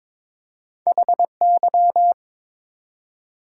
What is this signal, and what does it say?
Morse code, heard as a keyed tone, about 22 wpm, 700 Hz. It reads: HY
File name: Two Letter-Number Combinations - 22wpm